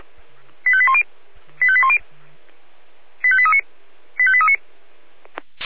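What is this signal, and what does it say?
Five-tone selective call (ZVEI), heard as quick runs of five tones, tones of about 70 ms. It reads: 85810 85810 85839 85839
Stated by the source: Five-tone ZVEI selective call received on an SDR.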